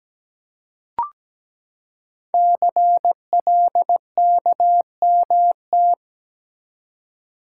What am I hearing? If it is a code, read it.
Morse code, heard as a keyed tone, about 17 wpm, 700 Hz. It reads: CLKMT
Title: Code Groups - 5 Letters - 17wpm